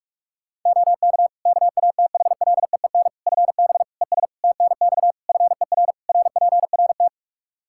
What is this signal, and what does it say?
Morse code, heard as a keyed tone, about 46 wpm, 700 Hz. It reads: OK KATHLEEN FB ES TNX FER RPRT
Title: Practice QSO - Exchange 2 - Info - 45wpm